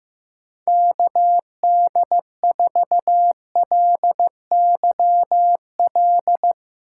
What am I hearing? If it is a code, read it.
Morse code, heard as a keyed tone, about 15 wpm, 700 Hz. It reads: KD4LYL
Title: USA Call Signs - 2x Character Spacing - 15wpm